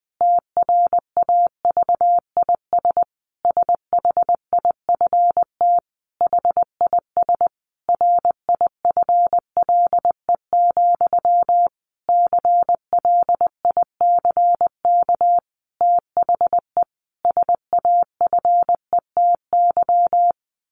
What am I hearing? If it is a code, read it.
Morse code, heard as a keyed tone, about 20 wpm, 700 Hz. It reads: TRAVIS SHIFT HIS RIFLE, CLICK THE SAFETY
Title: A Sound Of Thunder 20wpm